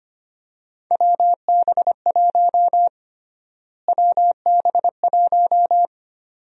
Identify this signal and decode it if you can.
Morse code, heard as a keyed tone, about 25 wpm, 700 Hz. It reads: WB1 WB1
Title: USA Call Sign Prefixes - Repeated 2x - 25wpm